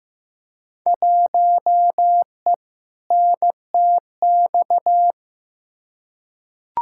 Morse code, heard as a keyed tone, about 15 wpm, 700 Hz. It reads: 1E NTX
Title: Field Day Exchanges - 15wpm